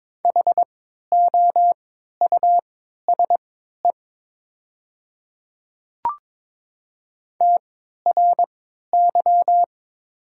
Morse code, heard as a keyed tone, about 22 wpm, spaced as for 13 wpm, 700 Hz. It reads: HOUSE TRY
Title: 3x Character Spacing - Top 200 Words - 22wpm